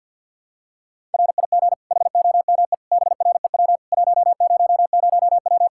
Morse code, heard as a keyed tone, about 50 wpm, 700 Hz. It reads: RIG HOME BREW 100W
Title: Practice QSO - Exchange 2 - Info - 50wpm